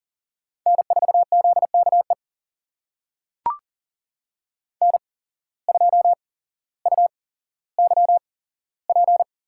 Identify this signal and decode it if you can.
Morse code, heard as a keyed tone, about 37 wpm, spaced as for 17 wpm, 700 Hz. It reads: N4ZKE N2UYP
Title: USA Call Signs - 5 Characters - 8x Character Spacing - 40wpm